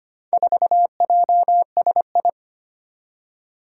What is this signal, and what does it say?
Morse code, heard as a keyed tone, about 25 wpm, 700 Hz. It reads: 4JSI
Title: Four Challenging Letter-Number Combinations 25wpm